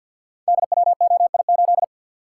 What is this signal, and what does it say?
Morse code, heard as a keyed tone, about 50 wpm, 700 Hz. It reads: DWOI8
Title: Code Groups - 5 Letter:Numbers - 50wpm